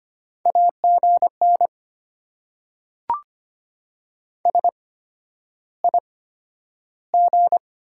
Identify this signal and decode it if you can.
Morse code, heard as a keyed tone, about 25 wpm, spaced as for 11 wpm, 690 Hz. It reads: AGN SIG
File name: Top 90 QSO Elements - 8x Character Spacing - 25wpm